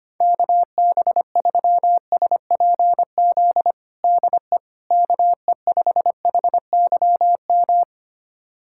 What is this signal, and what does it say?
Morse code, heard as a keyed tone, about 25 wpm, 700 Hz. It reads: KB3SPZ DE KE5HYM
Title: USA Callsign DE Callsign - 6 Characters - 25wpm